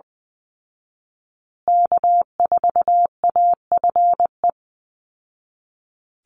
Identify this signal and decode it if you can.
Morse code, heard as a keyed tone, about 20 wpm, 700 Hz. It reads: K4AFE
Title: Call Signs Repeated 2x - 20wpm